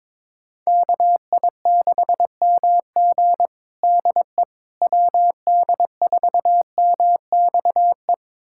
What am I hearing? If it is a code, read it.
Morse code, heard as a keyed tone, about 22 wpm, 700 Hz. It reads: KI6MG DE WD4MXE
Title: USA Callsign DE Callsign 22wpm